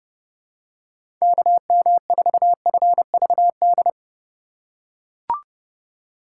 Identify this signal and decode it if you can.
Morse code, heard as a keyed tone, about 30 wpm, 700 Hz. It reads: KM4FVD
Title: USA Call Signs - 6 Characters - 3x Character Spacing - 30wpm